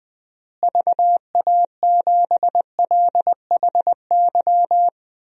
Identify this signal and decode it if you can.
Morse code, heard as a keyed tone, about 20 wpm, 700 Hz. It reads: VA7LHY
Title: DXCC-100 Call Signs - 6 Characters - Repeated 3x - 20wpm